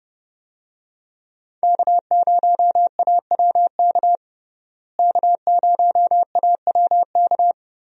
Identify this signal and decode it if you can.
Morse code, heard as a keyed tone, about 30 wpm, 700 Hz. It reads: K0AWK K0AWK
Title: USA Call Signs - 5 Characters - Repeated 3x - 30wpm